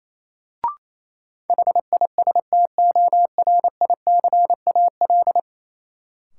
Morse code, heard as a keyed tone, about 28 wpm, 700 Hz. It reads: HISTORICAL
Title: Sentences from Top 600 Words - New Word Drill - 28wpm